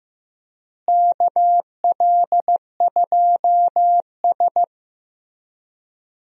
Morse code, heard as a keyed tone, about 15 wpm, 700 Hz. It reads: KL2S
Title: USA Call Signs - 4 Characters - Repeated 2x - 15wpm